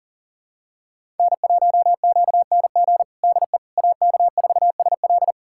Morse code, heard as a keyed tone, about 40 wpm, 700 Hz. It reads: N1QNG DE AK4SL
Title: USA Callsign DE Callsign - 5 Characters - 40wpm